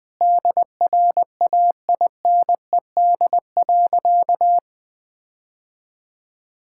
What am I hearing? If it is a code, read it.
Morse code, heard as a keyed tone, about 20 wpm, 700 Hz. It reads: DRAINED.
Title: All-Age Appropriate Jokes - 20wpm